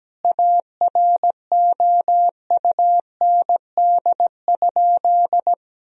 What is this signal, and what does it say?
Morse code, heard as a keyed tone, about 17 wpm, 700 Hz. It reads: AROUND?
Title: Sentences from Top 1000 Words - Review Part 4 - 17wpm